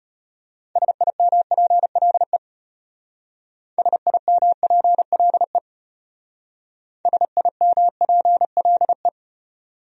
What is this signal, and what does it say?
Morse code, heard as a keyed tone, about 32 wpm, 700 Hz. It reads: SIMPLE SIMPLE SIMPLE